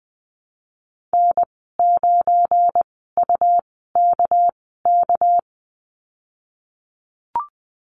Morse code, heard as a keyed tone, about 20 wpm, spaced as for 14 wpm, 700 Hz. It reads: N9UKK
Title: 213 Lesson 34g - Number 8 - Call Signs 20wpm